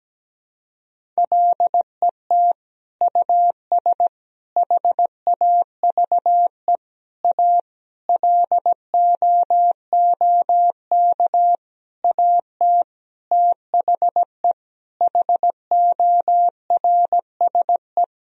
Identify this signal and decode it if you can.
Morse code, heard as a keyed tone, about 17 wpm, 700 Hz. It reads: LET US HAVE A LOOK AT THE HORSE
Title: Sentences from Top 1000 Words - Review Part 1 - 17wpm